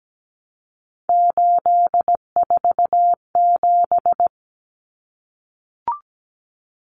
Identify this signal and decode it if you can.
Morse code, heard as a keyed tone, about 17 wpm, 700 Hz. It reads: 847
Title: Numbers - 3-Digit 17wpm